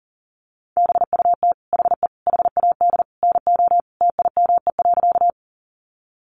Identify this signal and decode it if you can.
Morse code, heard as a keyed tone, about 40 wpm, 700 Hz. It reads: BUT HE HAD NO TIME.
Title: Sentences from Top 100 Words 40wpm